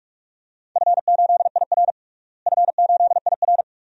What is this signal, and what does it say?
Morse code, heard as a keyed tone, about 45 wpm, 700 Hz. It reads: F8IR F8IR
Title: DXCC-100 Call Signs - 4 Characters - Repeated 2x - 45wpm